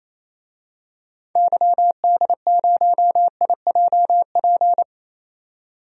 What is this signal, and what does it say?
Morse code, heard as a keyed tone, about 28 wpm, 700 Hz. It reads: YD0IJP
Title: DXCC-100 Call Signs - 6 Characters - 28wpm